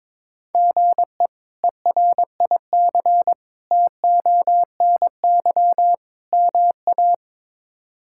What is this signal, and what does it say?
Morse code, heard as a keyed tone, about 21 wpm, 700 Hz. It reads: GE ERIC TONY MA
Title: K1USN SST QSO Practice - 22wpm